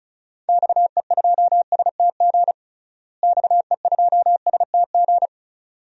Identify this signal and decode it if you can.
Morse code, heard as a keyed tone, about 36 wpm, 700 Hz. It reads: XE2STG XE2STG
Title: DXCC-100 Call Signs - 6 Characters - Repeated 2x - 35wpm